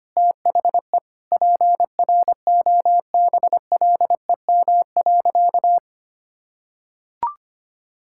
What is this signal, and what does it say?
Morse code, heard as a keyed tone, about 25 wpm, 700 Hz. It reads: THE PROBLEM.
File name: practice / 25wpm / Sentences with 9-Word Phrases - 25wpm